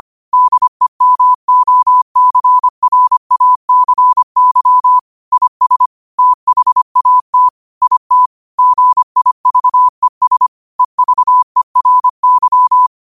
Morse code, heard as a keyed tone, about 25 wpm, 1000 Hz. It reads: DEMOCRACY IS THAT IT GIVES EVERY
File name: qotdfiles / 25wpm